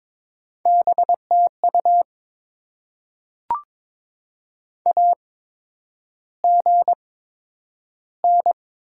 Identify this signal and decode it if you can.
Morse code, heard as a keyed tone, about 22 wpm, spaced as for 9 wpm, 700 Hz. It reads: BTU AGN
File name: Top 20 QSO Elements - 8x Character Spacing - 22wpm